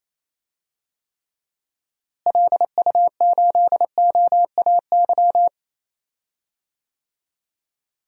Morse code, heard as a keyed tone, about 28 wpm, 700 Hz. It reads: LU8OAY
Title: DXCC-100 Call Signs - 6 Characters - Repeated 3x - 28wpm